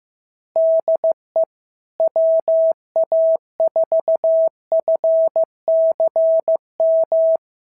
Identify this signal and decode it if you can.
Morse code, heard as a keyed tone, about 14 wpm, 650 Hz. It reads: DE WA4FCM
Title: SOTA QSO Practice 15wpm